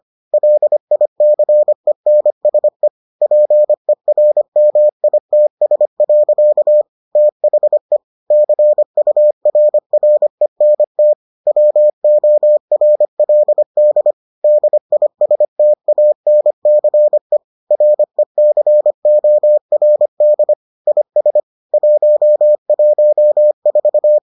Morse code, heard as a keyed tone, about 25 wpm, 600 Hz. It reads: LICENSE PERMITS. THE CURRENT WORLD DISTANCE RECORD IS 114